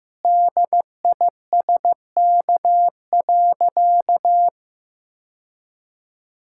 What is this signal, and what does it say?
Morse code, heard as a keyed tone, about 15 wpm, 700 Hz. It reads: DISK.
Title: Sentences from Top 1000 Words - Review Part 6 - 15wpm